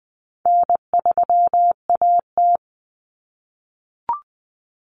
Morse code, heard as a keyed tone, about 20 wpm, 700 Hz. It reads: N3AT
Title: Common Cut Numbers - 4-Digit 20wpm